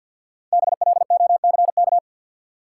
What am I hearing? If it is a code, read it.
Morse code, heard as a keyed tone, about 53 wpm, 700 Hz. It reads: BLOCK
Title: Sentences from Top 1600 Words - New Word Drill - 50wpm